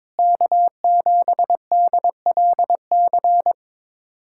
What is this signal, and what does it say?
Morse code, heard as a keyed tone, about 22 wpm, 700 Hz. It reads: K7DLC
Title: USA Callsign DE Callsign - 5 Characters - 22wpm